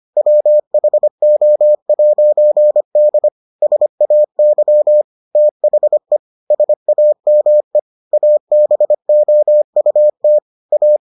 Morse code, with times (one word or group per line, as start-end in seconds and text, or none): 0.2-3.3 WHO'D
3.6-5.0 SAY
5.4-6.2 THE
6.5-7.8 SAME
8.1-10.4 ABOUT
10.7-11.0 A